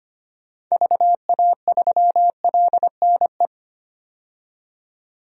Morse code, heard as a keyed tone, about 25 wpm, 700 Hz. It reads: VA3LNE
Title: DXCC-100 Call Signs - 6 Characters - 25wpm